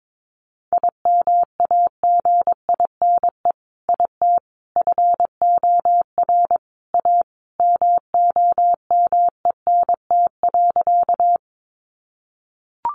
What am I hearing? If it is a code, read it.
Morse code, heard as a keyed tone, about 22 wpm, 700 Hz. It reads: IMAGINE IT FOR A MOMENT.
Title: Sentences from Top 1400 Words 22wpm